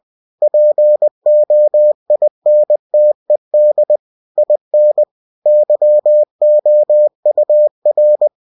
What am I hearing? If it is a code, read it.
Morse code, heard as a keyed tone, about 20 wpm, 600 Hz. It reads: POINTED IN YOUR